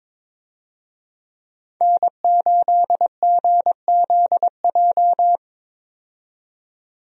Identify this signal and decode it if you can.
Morse code, heard as a keyed tone, about 22 wpm, 700 Hz. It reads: N8GZJ
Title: USA Call Signs - 5 Characters - 22wpm